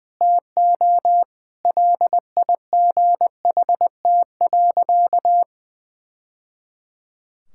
Morse code, heard as a keyed tone, about 19 wpm, 700 Hz. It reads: TO LIGHT.
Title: Sentences from Top 1000 Words - Review Part 5 - 20wpm